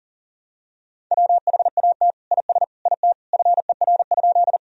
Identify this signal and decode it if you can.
Morse code, heard as a keyed tone, about 40 wpm, 700 Hz. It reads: WHAT IS IT FER?
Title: Sentences from Top 100 Words and Abbreviations 40wpm